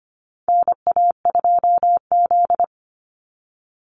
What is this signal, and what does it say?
Morse code, heard as a keyed tone, about 25 wpm, 700 Hz. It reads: NA2Z
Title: USA Call Signs 25wpm